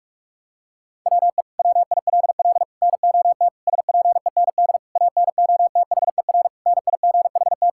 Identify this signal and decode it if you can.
Morse code, heard as a keyed tone, about 42 wpm, 700 Hz. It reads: WE WILL NOT SPEND ANOTHER NIGHT
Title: Sentences from Top 1000 Words - Review Part 5 - 45wpm